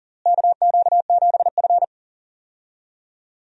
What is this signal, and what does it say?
Morse code, heard as a keyed tone, about 40 wpm, 700 Hz. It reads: KQ7F
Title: USA Call Signs - 4 Characters - Repeated 2x - 40wpm